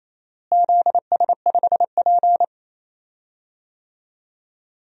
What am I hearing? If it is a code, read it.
Morse code, heard as a keyed tone, about 27 wpm, 700 Hz. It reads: ZS5P
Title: DXCC-100 Call Signs - 4 Characters - Repeated 2x - 28wpm